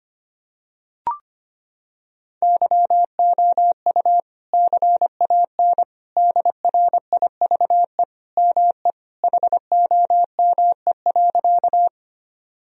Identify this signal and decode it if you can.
Morse code, heard as a keyed tone, about 25 wpm, 700 Hz. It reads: YOU CAN DRIVE ME HOME.
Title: Sentences from Top 1000 Words - Review Part 2 - 25wpm